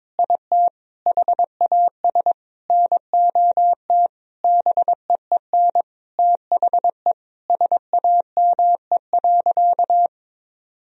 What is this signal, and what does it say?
Morse code, heard as a keyed tone, about 22 wpm, 700 Hz. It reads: IT HAS NOT BEEN THE SAME.
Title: Sentences from Top 1000 Words - Review Part 1 - 22wpm